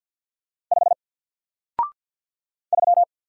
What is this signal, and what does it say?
Morse code, heard as a keyed tone, about 48 wpm, 700 Hz. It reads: 5 3
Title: Single Letter-Number Alt 50wpm